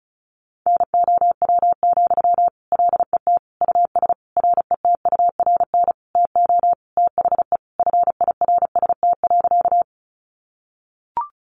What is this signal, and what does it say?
Morse code, heard as a keyed tone, about 34 wpm, 700 Hz. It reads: NOW, LET US RETURN TO THE FIRST.
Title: Sentences from Top 800 Words 35wpm